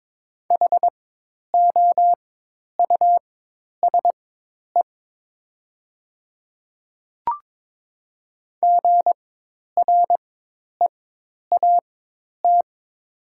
Morse code, heard as a keyed tone, about 22 wpm, spaced as for 10 wpm, 700 Hz. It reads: HOUSE GREAT
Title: 4x Character Spacing - Top 200 Words - 22wpm